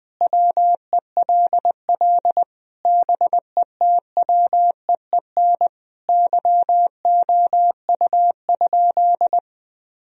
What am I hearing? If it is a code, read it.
Morse code, heard as a keyed tone, about 20 wpm, 700 Hz. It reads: WELL BETWEEN YOU?
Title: Sentences from Top 1000 Words - Review Part 1 - 20wpm